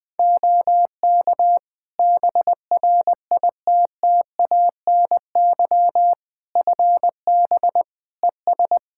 Morse code, heard as a keyed tone, about 21 wpm, 700 Hz. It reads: OK BRITTANY FB ES
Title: Practice QSO - Exchange 2 - Info - 20wpm